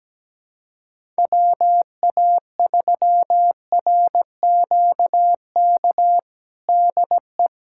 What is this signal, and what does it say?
Morse code, heard as a keyed tone, about 17 wpm, 700 Hz. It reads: WA3RQK DE
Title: USA Callsign DE Callsign - 6 Characters - 17wpm